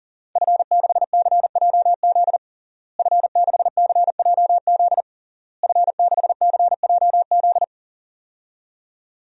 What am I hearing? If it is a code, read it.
Morse code, heard as a keyed tone, about 40 wpm, 700 Hz. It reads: F6CJZ F6CJZ F6CJZ
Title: DXCC-100 Call Signs - 5 Characters - Repeated 3x - 40wpm